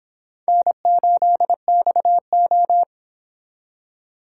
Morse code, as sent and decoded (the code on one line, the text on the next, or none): -. ---.. -..- ---
N8XO